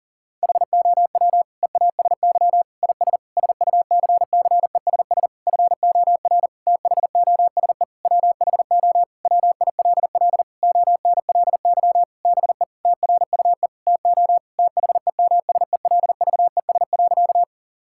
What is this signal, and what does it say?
Morse code, heard as a keyed tone, about 40 wpm, 700 Hz. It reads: HOW EASY IS SUCCESS FOR THOSE WHO WILL ONLY BE TRUE TO THEMSELVES.
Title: Sentences from Top 1000 Words - Review Part 6 - 40wpm